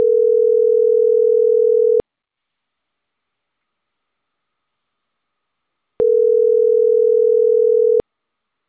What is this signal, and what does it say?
Ringback tone (North America), 440 + 480 Hz.